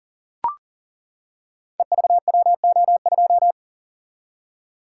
Morse code, heard as a keyed tone, about 39 wpm, 700 Hz. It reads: EVWO2
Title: Code Groups - 5 Letter:Numbers - 40wpm